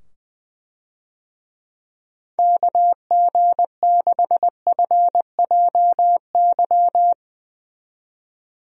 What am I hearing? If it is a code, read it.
Morse code, heard as a keyed tone, about 20 wpm, 700 Hz. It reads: KG6FJY
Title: USA Call Signs - 6 Characters - 20wpm